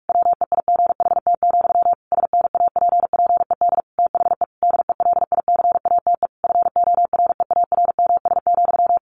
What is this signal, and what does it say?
Morse code, heard as a keyed tone, about 44 wpm, 700 Hz. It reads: WEIGHT, SNAPPED THE DELICATE FOREARMS,
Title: A Sound Of Thunder 45wpm